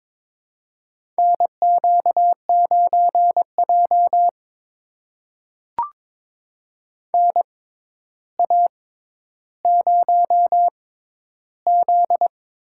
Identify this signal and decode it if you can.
Morse code, heard as a keyed tone, about 22 wpm, spaced as for 11 wpm, 700 Hz. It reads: NQ9J NA0Z
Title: USA Call Signs - 4 Characters - 6x Character Spacing - 22wpm